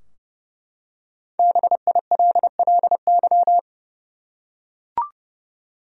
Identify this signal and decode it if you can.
Morse code, heard as a keyed tone, about 30 wpm, 700 Hz. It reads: BILLY
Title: USA Common Names - Max 5 Letters - 30wpm